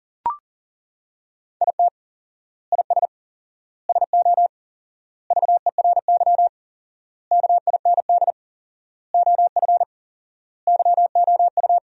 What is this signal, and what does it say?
Morse code, heard as a keyed tone, about 37 wpm, 700 Hz. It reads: IT IS SO VERY KIND OF YOU
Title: 4x Word-Spacing - Sentences from Top 300 Words - 40wpm